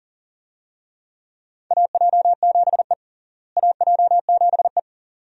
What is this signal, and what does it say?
Morse code, heard as a keyed tone, about 40 wpm, 700 Hz. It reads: AJ7E AJ7E
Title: USA Call Signs - 4 Characters - Repeated 2x - 40wpm